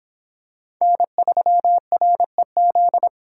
Morse code, heard as a keyed tone, about 26 wpm, 700 Hz. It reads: N3REZ